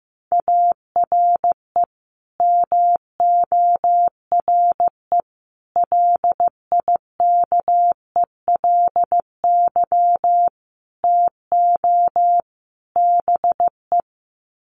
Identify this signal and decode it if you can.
Morse code, heard as a keyed tone, about 15 wpm, 700 Hz. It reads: ARE MORE LIKELY TO BE
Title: Sets of 5 Words 15wpm